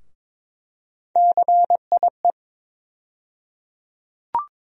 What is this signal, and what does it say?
Morse code, heard as a keyed tone, about 22 wpm, 700 Hz. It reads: CIE
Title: USA Call Sign Postfixes - 3 Letters 22wpm